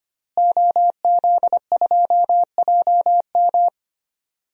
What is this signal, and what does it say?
Morse code, heard as a keyed tone, about 25 wpm, 700 Hz. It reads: OZ2JM
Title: DXCC-100 Call Signs - 5 Characters - Repeated 3x - 25wpm